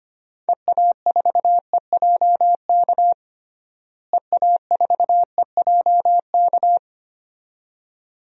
Morse code, heard as a keyed tone, about 25 wpm, 700 Hz. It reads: EA4EJK EA4EJK
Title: DXCC-100 Call Signs - 6 Characters - Repeated 2x - 25wpm